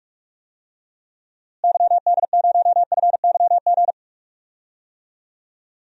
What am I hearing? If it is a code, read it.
Morse code, heard as a keyed tone, about 44 wpm, 700 Hz. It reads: YD0FYG
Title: DXCC-100 Call Signs - 6 Characters - Repeated 2x - 45wpm